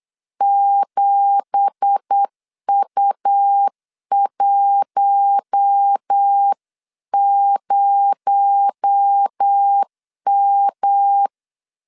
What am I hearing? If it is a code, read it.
Morse code, heard as a keyed tone, about 8 wpm, 800 Hz. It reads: 7U10M
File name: LEZIONI / 10 SIGNS / 10A SLASH 8wpm 3452